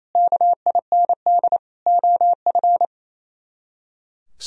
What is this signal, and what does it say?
Morse code, heard as a keyed tone, about 36 wpm, 700 Hz. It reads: KIND OF